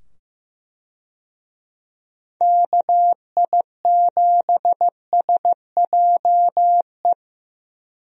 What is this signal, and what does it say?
Morse code, heard as a keyed tone, about 15 wpm, 700 Hz. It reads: KI7SJE
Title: USA Call Signs - 6 Characters - 15wpm